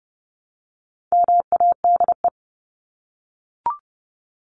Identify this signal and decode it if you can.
Morse code, heard as a keyed tone, about 30 wpm, 700 Hz. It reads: MADE
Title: Top 300 Words Repeated 30wpm